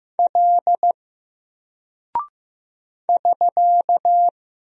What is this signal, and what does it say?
Morse code, heard as a keyed tone, about 15 wpm, 700 Hz. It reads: L <SK>
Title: Single Character (Letter:Number:Punctuation:Prosign) 15wpm